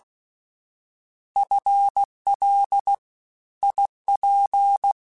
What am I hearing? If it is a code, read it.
Morse code, heard as a keyed tone, about 16 wpm, 800 Hz. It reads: FL IP
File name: FL / FL mp3s / Fast FL @15